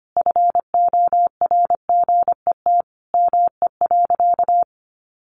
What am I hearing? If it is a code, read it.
Morse code, heard as a keyed tone, about 25 wpm, 700 Hz. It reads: FORGET ME.
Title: Sentences from Top 1200 Words 25wpm